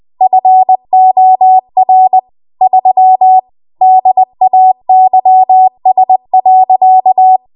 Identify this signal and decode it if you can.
Morse code, heard as a keyed tone, about 20 wpm, 750 Hz. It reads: FOR 3 DAYS.